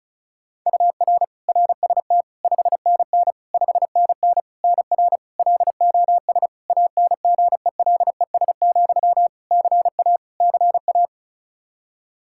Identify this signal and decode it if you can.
Morse code, heard as a keyed tone, about 34 wpm, 700 Hz. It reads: UR RST 5NN 5NN NR LOS ANGELES, CA CA
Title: K1USN SST QSO Practice 35wpm